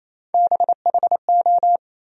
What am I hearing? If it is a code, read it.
Morse code, heard as a keyed tone, about 28 wpm, 700 Hz. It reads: BHO
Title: USA Call Sign Postfixes - 3 Letters 28wpm